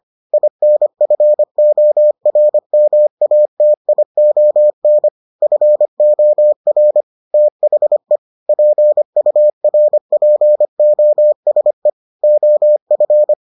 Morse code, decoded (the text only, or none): INFORMATION FOR THE PURPOSE OF